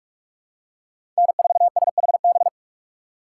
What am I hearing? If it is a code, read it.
Morse code, heard as a keyed tone, about 46 wpm, 700 Hz. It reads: N4SHB
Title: USA Call Signs - 4x Character Spacing - 45wpm